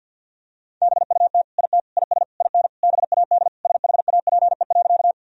Morse code, heard as a keyed tone, about 49 wpm, 700 Hz. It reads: BUT IT IS IN BAD SHAPE.
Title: Sentences from Top 1000 Words - Review Part 3 - 50wpm